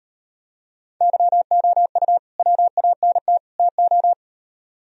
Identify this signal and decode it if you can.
Morse code, heard as a keyed tone, about 38 wpm, 700 Hz. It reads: YOU WANT TO